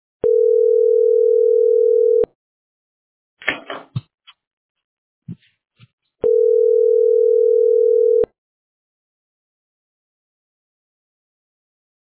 Ringback tone (North America), 440 + 480 Hz.